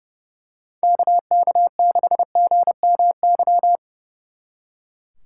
Morse code, heard as a keyed tone, about 28 wpm, 700 Hz. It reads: KK6GMY
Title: Speed Racing - USA 2x3 Call Signs 30wpm